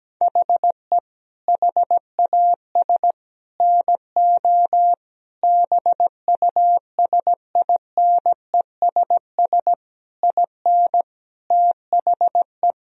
Morse code, heard as a keyed tone, about 17 wpm, 700 Hz. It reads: HE HAS NO BUSINESS IN THE
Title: Sentences from Top 1000 Words - Review Part 4 - 17wpm